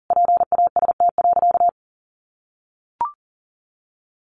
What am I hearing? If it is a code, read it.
Morse code, heard as a keyed tone, about 40 wpm, 700 Hz. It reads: PAST.
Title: Sentences from Top 1200 Words 40wpm